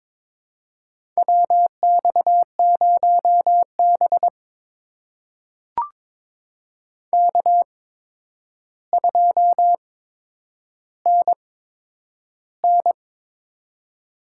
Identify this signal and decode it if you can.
Morse code, heard as a keyed tone, about 22 wpm, spaced as for 9 wpm, 700 Hz. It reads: WX0B K2NN
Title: USA Call Signs - 4 Characters - 8x Character Spacing - 22wpm